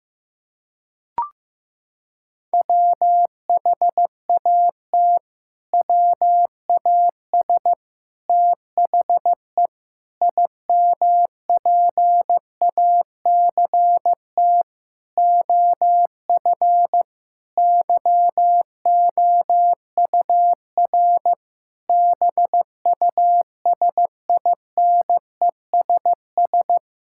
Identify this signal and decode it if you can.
Morse code, heard as a keyed tone, about 15 wpm, 700 Hz. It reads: WHAT WAS THE IMPACT OF YOUR BUSINESS
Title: Sentences from Top 1000 Words - Review Part 6 - 15wpm